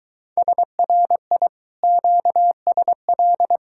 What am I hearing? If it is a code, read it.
Morse code, heard as a keyed tone, about 22 wpm, 700 Hz. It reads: SRI QSL